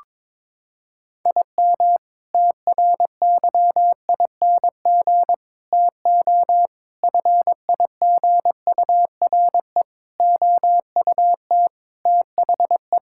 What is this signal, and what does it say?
Morse code, heard as a keyed tone, about 22 wpm, 700 Hz. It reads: IM TRYING TO FIGURE OUT THE
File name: practice / 22wpm / Sentences with 4-Word Phrases - 22wpm